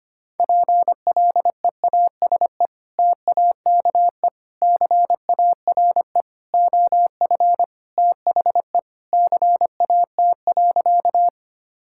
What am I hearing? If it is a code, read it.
Morse code, heard as a keyed tone, about 25 wpm, 700 Hz. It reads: PLEASE TAKE CARE OF THE CAT.
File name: practice / 25wpm / Sentences with 3-Word Phrases - 25wpm